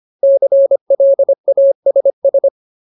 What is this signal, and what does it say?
Morse code, heard as a keyed tone, about 25 wpm, 550 Hz. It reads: CLASS
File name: Ultimate Word List / 25WPM Ultimate